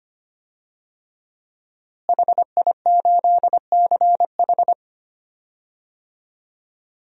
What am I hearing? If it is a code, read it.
Morse code, heard as a keyed tone, about 25 wpm, 700 Hz. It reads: HI8CH
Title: DXCC-100 Call Signs - 5 Characters - Repeated 3x - 25wpm